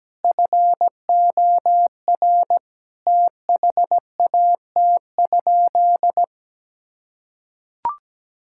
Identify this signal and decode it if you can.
Morse code, heard as a keyed tone, about 17 wpm, 700 Hz. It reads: FOR THAT?
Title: Sentences from Top 1000 Words - Review Part 6 - 17wpm